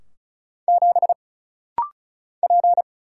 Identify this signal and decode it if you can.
Morse code, heard as a keyed tone, about 36 wpm, 700 Hz. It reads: 7 P
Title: Single Letter-Number Alt - Rapid-Fire 35wpm